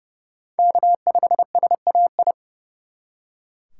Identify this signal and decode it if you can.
Morse code, heard as a keyed tone, about 37 wpm, 700 Hz. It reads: K5SAI
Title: Speed Racing - USA 1x3 Call Signs 30wpm